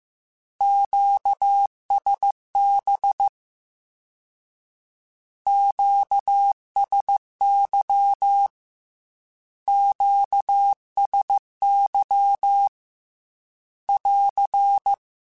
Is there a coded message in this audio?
Yes: Morse code from 0.6 to 14.9 s, heard as a keyed tone, about 15 wpm, 800 Hz. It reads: QSB QSY QSY <AR>